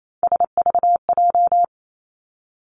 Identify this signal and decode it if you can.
Morse code, heard as a keyed tone, about 28 wpm, 700 Hz. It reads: SVJ
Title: Three Challenging Letter-Number Combinations 28wpm